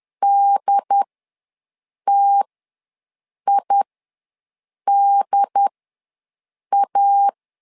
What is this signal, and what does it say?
Morse code, heard as a keyed tone, about 10 wpm, spaced as for 6 wpm, 800 Hz. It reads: DTIDA
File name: LEZIONI / 2 UHDN / 2 UHD ws 10wpm 2058